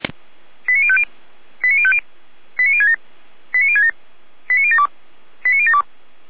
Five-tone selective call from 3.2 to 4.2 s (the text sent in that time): 89067